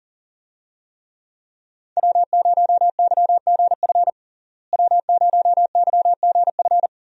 Morse code, heard as a keyed tone, about 40 wpm, 700 Hz. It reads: W0YGF W0YGF
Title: USA Call Signs - 5 Characters - Repeated 2x - 40wpm